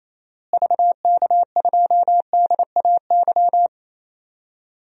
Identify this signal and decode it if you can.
Morse code, heard as a keyed tone, about 28 wpm, 700 Hz. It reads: VK2DAY
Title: DXCC-100 Call Signs - 6 Characters - Repeated 3x - 28wpm